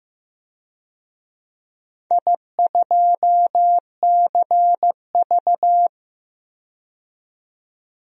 Morse code, heard as a keyed tone, about 15 wpm, 700 Hz. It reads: I2CV
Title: DXCC-100 Call Signs - 4 Characters - 15wpm